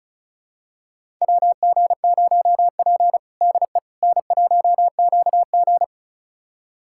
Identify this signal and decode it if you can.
Morse code, heard as a keyed tone, about 34 wpm, 700 Hz. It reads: WG0P DE N1QG
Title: USA Callsign DE Callsign - 4 Characters - 35wpm